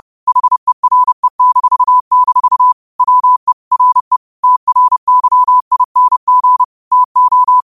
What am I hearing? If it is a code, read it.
Morse code, heard as a keyed tone, about 30 wpm, 1000 Hz. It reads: HERE-- WE RE TRYING TO
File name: qotdfiles / 30wpm